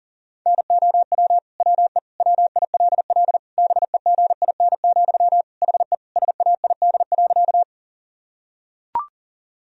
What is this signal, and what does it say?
Morse code, heard as a keyed tone, about 40 wpm, 700 Hz. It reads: NOW WE WILL BEGIN, HE SAID.
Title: Sentences from Top 1000 Words - Review Part 1 - 40wpm